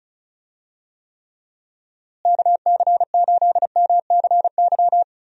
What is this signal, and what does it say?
Morse code, heard as a keyed tone, about 36 wpm, 700 Hz. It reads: KC8MCY
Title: USA Call Signs - 6 Characters - Repeated 3x - 35wpm